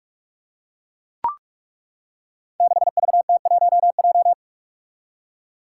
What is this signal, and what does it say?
Morse code, heard as a keyed tone, about 42 wpm, 700 Hz. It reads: 6VT1J
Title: Five Challenging Letter-Number Combinations 45wpm